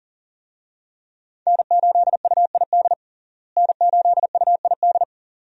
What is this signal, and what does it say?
Morse code, heard as a keyed tone, about 40 wpm, 700 Hz. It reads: N8UID N8UID
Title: USA Call Signs - 5 Characters - Repeated 2x - 40wpm